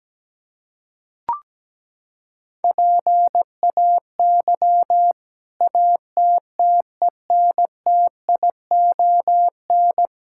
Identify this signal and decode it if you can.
Morse code, heard as a keyed tone, about 17 wpm, 700 Hz. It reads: PAY ATTENTION
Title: Sentences from Top 1000 Words - Review Part 6 - 17wpm